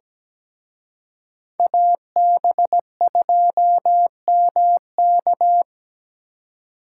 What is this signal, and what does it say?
Morse code, heard as a keyed tone, about 20 wpm, 700 Hz. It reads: AB2MK
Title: Speed Racing - USA 2x2 Call Signs 17wpm